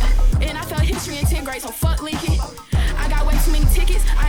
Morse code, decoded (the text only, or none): none